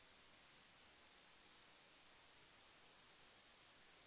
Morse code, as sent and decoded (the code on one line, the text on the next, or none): none